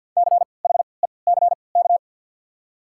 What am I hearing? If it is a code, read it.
Morse code, heard as a keyed tone, about 52 wpm, spaced as for 29 wpm, 690 Hz. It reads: CHECK
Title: 3x Character Spacing - Top 500 Words - 50wpm